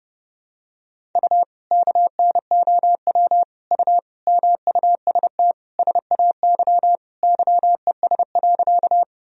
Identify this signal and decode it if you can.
Morse code, heard as a keyed tone, about 30 wpm, 700 Hz. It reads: U KNOW U MUST SAY YES.
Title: Sentences from Top 500 Words and Abbreviations 30wpm